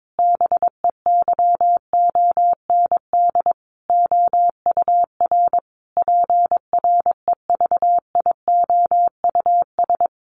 Morse code, heard as a keyed tone, about 22 wpm, 700 Hz. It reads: BEYOND OUR PREVIOUS
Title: Sentences from Top 1500 Words 22wpm